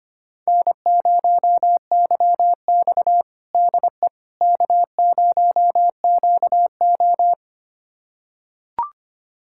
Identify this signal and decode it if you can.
Morse code, heard as a keyed tone, about 25 wpm, 700 Hz. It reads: N0YX DE K0QO
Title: USA Callsign DE Callsign - 4 Characters - 25wpm